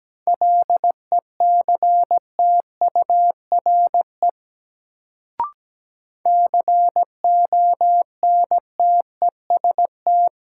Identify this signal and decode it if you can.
Morse code, heard as a keyed tone, about 17 wpm, 700 Hz. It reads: LECTURE CONTEST
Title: Sentences from Top 1600 Words - New Word Drill - 17wpm